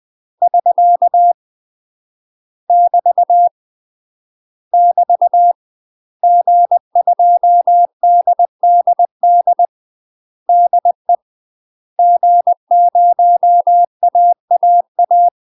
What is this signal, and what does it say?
Morse code, heard as a keyed tone, about 20 wpm, 700 Hz. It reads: <SK> <BT> <BT> G2DDD DE G0AAA
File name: UK MASTER QSOs TRAINING/20WPM